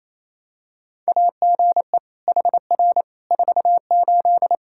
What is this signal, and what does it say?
Morse code, heard as a keyed tone, about 28 wpm, 700 Hz. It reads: AGE HR 48
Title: Practice QSO - Exchange 3 - More Info - 28wpm